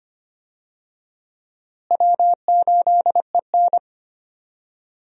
Morse code, heard as a keyed tone, about 25 wpm, 700 Hz. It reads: W8EN
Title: USA Call Signs - 4 Characters - Repeated 3x - 25wpm